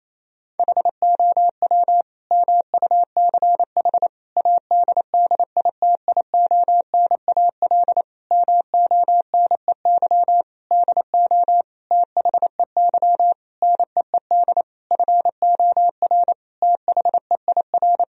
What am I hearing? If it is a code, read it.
Morse code, heard as a keyed tone, about 28 wpm, 700 Hz. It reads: HOW MUCH ADDITIONAL MONEY DO THEY NEED FOR THEIR
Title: Sentences from Top 1000 Words - Review Part 4 - 28wpm